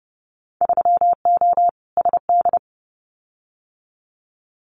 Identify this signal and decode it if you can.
Morse code, heard as a keyed tone, about 30 wpm, 700 Hz. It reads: 3O SD
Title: Winter Field Day Exchanges 30wpm